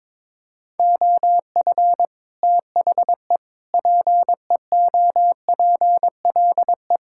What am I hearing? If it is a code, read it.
Morse code, heard as a keyed tone, about 22 wpm, 700 Hz. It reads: OF THE PEOPLE
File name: practice / 22wpm / Sets of 8 Words - 22wpm